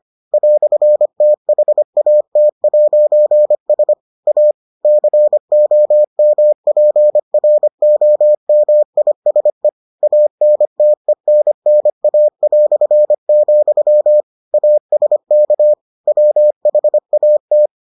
Morse code, heard as a keyed tone, about 25 wpm, 600 Hz. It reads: "THAT'S A COMPROMISE ANTENNA", ASK WHAT